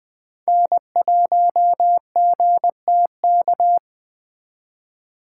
Morse code, heard as a keyed tone, about 20 wpm, 700 Hz. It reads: N1GTK
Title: Speed Racing - USA 1x3 Call Signs 17wpm